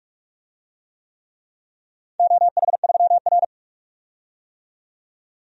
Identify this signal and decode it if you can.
Morse code, heard as a keyed tone, about 46 wpm, 700 Hz. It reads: OH3R
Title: DXCC-100 Call Signs - 4 Characters - 45wpm